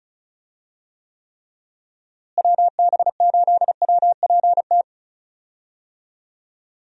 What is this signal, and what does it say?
Morse code, heard as a keyed tone, about 34 wpm, 700 Hz. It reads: WB8WPT
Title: USA Call Signs - 6 Characters - Repeated 3x - 35wpm